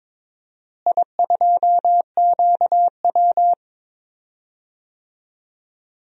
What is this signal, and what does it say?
Morse code, heard as a keyed tone, about 22 wpm, 700 Hz. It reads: I2QW
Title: DXCC-100 Call Signs - 4 Characters - Repeated 2x - 22wpm